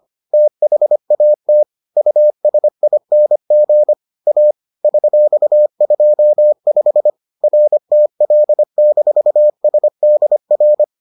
Morse code, heard as a keyed tone, about 25 wpm, 600 Hz. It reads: THAT USING A $25 RTL-SDR